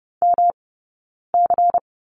Morse code, heard as a keyed tone, about 30 wpm, spaced as for 9 wpm, 690 Hz. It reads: MC
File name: Lesson 18d - Letter m - Two Characters 30wpm